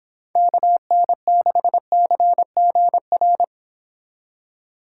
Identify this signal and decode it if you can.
Morse code, heard as a keyed tone, about 25 wpm, 700 Hz. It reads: KN6CGR
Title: Speed Racing - USA 2x3 Call Signs 22wpm